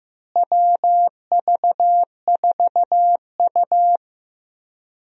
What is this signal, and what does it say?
Morse code, heard as a keyed tone, about 15 wpm, 700 Hz. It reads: WV4U
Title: USA Call Signs - 4 Characters - Repeated 2x - 15wpm